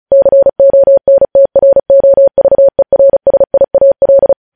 Morse code, heard as a keyed tone, about 36 wpm, 550 Hz. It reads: CONTROVERSIAL
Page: News headlines in Morse code at 35 wpm